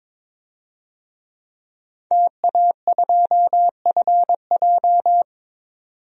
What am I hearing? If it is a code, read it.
Morse code, heard as a keyed tone, about 22 wpm, 700 Hz. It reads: TA2FJ
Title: DXCC-100 Call Signs - 5 Characters - 22wpm